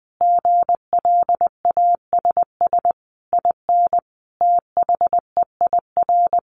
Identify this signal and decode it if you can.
Morse code, heard as a keyed tone, about 20 wpm, 700 Hz. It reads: GLASS IN THEIR
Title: Sentences from Top 1300 Words 20wpm